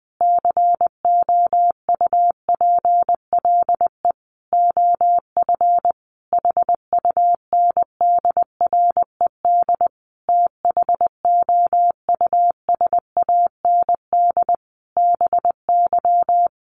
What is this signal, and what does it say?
Morse code, heard as a keyed tone, about 20 wpm, 700 Hz. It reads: COUPLE OF HUNDRED THOUSAND BY
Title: Sentences from Top 1100 Words 20wpm